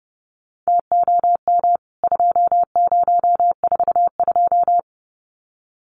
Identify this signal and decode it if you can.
Morse code, heard as a keyed tone, about 30 wpm, 700 Hz. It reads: TOM 2042
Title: CWT Exchange 30wpm